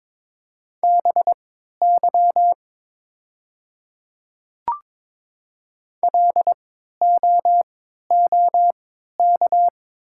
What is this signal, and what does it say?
Morse code, heard as a keyed tone, about 23 wpm, spaced as for 13 wpm, 700 Hz. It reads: BY LOOK
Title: 3x Character Spacing - Top 100 Words - 22wpm